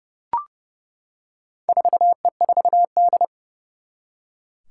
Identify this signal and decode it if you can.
Morse code, heard as a keyed tone, about 32 wpm, 700 Hz. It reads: 4E4D
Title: Four Challenging Letter-Number Combinations 30wpm